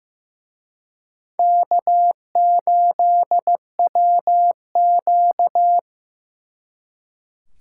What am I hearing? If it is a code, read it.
Morse code, heard as a keyed tone, about 18 wpm, 700 Hz. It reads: K8WQ
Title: Speed Racing - USA Call Signs 15wpm